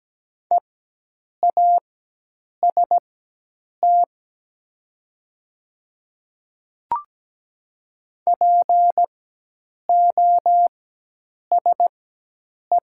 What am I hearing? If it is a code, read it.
Morse code, heard as a keyed tone, about 17 wpm, spaced as for 8 wpm, 700 Hz. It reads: EAST POSE
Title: 4x Character Spacing - Top 500 Words - 17wpm